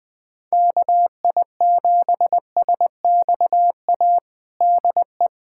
Morse code, heard as a keyed tone, about 21 wpm, 700 Hz. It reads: KI7SXA DE
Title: USA Callsign DE Callsign - 6 Characters - 20wpm